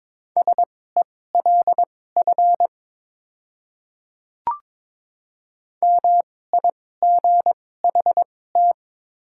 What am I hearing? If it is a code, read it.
Morse code, heard as a keyed tone, about 22 wpm, spaced as for 16 wpm, 700 Hz. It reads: SELF MIGHT
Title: 2x Character Spacing - Top 500 Words - 22wpm